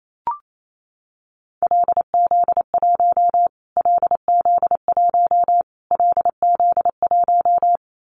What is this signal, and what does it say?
Morse code, heard as a keyed tone, about 28 wpm, 700 Hz. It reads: LZ1 LZ1 LZ1
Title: Callsign Prefixes for Top 25 DXCC - Repeated 3x - 28wpm